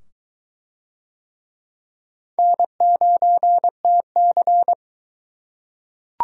Morse code, heard as a keyed tone, about 22 wpm, 700 Hz. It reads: N9TC